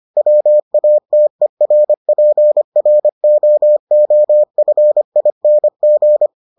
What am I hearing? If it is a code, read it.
Morse code, heard as a keyed tone, about 25 wpm, 600 Hz. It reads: WATERPROOFING